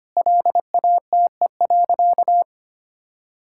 Morse code, heard as a keyed tone, about 25 wpm, 700 Hz. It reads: LATE.
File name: practice / 25wpm / Sentences with 4-Word Phrases - 25wpm